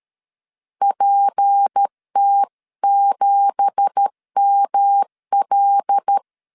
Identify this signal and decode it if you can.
Morse code, heard as a keyed tone, about 13 wpm, 800 Hz. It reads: PT7ML
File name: CW / DET / 1 9 1500C S B 12wpm 3819